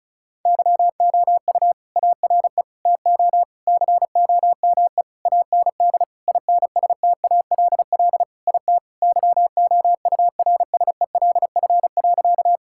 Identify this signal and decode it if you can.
Morse code, heard as a keyed tone, about 36 wpm, 700 Hz. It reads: YOU ARE TO COME AND INSTALL IT YOURSELF.
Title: Sentences from Top 1600 Words 35wpm